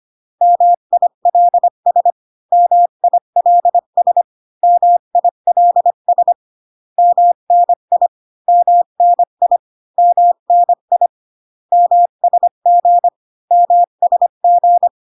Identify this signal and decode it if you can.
Morse code, heard as a keyed tone, about 25 wpm, 700 Hz. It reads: MILS MILS MILS MNI MNI MNI MSG MSG